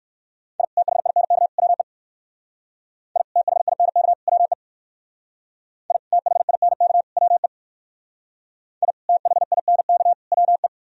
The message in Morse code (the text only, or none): I THINK WE I THINK WE I THINK WE I THINK WE